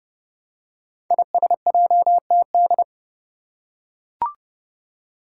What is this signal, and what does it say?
Morse code, heard as a keyed tone, about 30 wpm, 700 Hz. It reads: ISJTD
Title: Five Challenging Letter-Number Combinations 30wpm